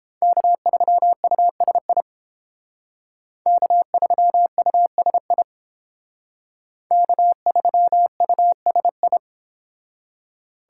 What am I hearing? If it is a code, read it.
Morse code, heard as a keyed tone, about 31 wpm, 700 Hz. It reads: K3USI K3USI K3USI